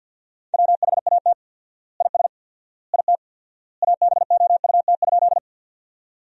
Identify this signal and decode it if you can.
Morse code, heard as a keyed tone, about 50 wpm, 700 Hz. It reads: WHAT IS IT ABOUT?
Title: 4x Word-Spacing - Sentences from Top 100 Words - 50wpm